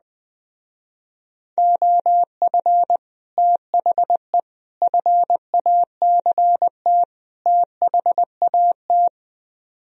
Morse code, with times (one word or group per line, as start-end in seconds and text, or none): 1.6-3.0 OF
3.4-4.4 THE
4.8-7.0 FACT
7.5-9.1 THAT